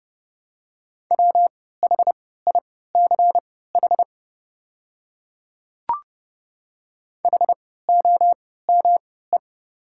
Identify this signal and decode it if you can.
Morse code, heard as a keyed tone, about 30 wpm, spaced as for 17 wpm, 700 Hz. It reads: WHICH HOME
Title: 3x Character Spacing - Top 200 Words - 30wpm